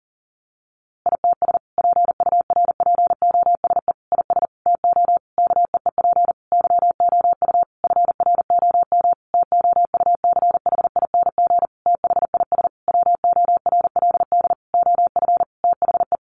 Morse code, heard as a keyed tone, about 40 wpm, 700 Hz. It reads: ITS PURPOSE IS TO KEEP YOU FROM TOUCHING THIS WORLD OF THE
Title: A Sound Of Thunder 40wpm